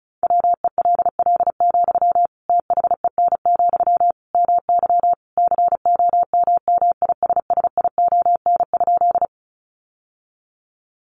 Morse code, heard as a keyed tone, about 34 wpm, 700 Hz. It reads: WELL, THEN, MY COMMISSION?
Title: Sentences from Top 1400 Words 35wpm